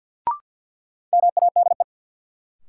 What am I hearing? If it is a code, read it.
Morse code, heard as a keyed tone, about 46 wpm, 700 Hz. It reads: MADE
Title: Sentences from Top 200 Words - New Word Drill - 50wpm